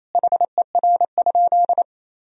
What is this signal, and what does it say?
Morse code, heard as a keyed tone, about 28 wpm, 700 Hz. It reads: HER?
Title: Sentences from Top 1000 Words - Review Part 4 - 28wpm